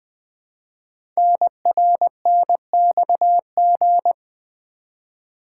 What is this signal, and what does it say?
Morse code, heard as a keyed tone, about 20 wpm, 700 Hz. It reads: NRNXG
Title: Code Groups - 5 Letters - 20wpm